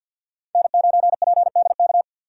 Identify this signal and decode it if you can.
Morse code, heard as a keyed tone, about 50 wpm, 700 Hz. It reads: N9PDK
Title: USA Call Signs - 5 Characters - 4x Character Spacing - 50wpm